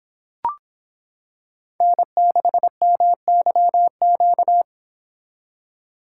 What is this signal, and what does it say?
Morse code, heard as a keyed tone, about 25 wpm, 700 Hz. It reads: N6MYQ